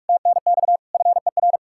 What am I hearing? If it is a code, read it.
Morse code, heard as a keyed tone, about 59 wpm, 700 Hz. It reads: TNX FER